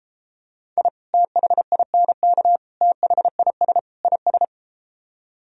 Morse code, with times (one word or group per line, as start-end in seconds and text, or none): 0.8-0.9 I
1.1-2.6 THINK
2.8-3.8 THIS
4.0-4.4 IS